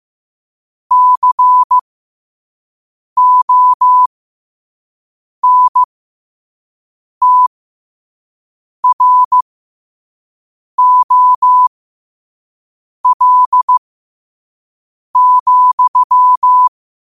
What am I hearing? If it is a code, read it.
Morse code, heard as a keyed tone, about 15 wpm, spaced as for 5 wpm, 1000 Hz. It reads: CONTROL,
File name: qotdfiles / 5wpm